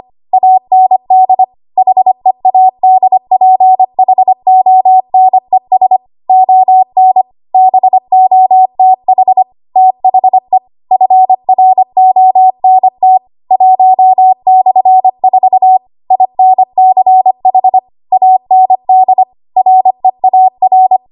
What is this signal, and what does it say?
Morse code, heard as a keyed tone, about 25 wpm, 750 Hz. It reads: AND HEADPHONES ON BOTH THE FRONT 1/4 INCH AND REAR
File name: Archive / 25 WPM / 25WPM